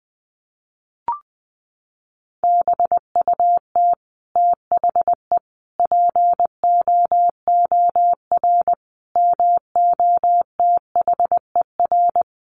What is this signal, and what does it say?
Morse code, heard as a keyed tone, about 20 wpm, 700 Hz. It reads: BUT THE POOR MOTHER
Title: Sentences from Top 1000 Words 20wpm